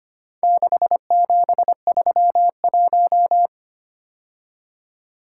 Morse code, heard as a keyed tone, about 25 wpm, 700 Hz. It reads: 6731
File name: practice / 25wpm / Four Letter-Number Combinations - 25wpm